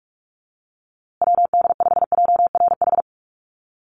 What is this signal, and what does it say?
Morse code, heard as a keyed tone, about 44 wpm, 700 Hz. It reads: WD5JRH
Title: USA Call Signs Repeated 2x 45wpm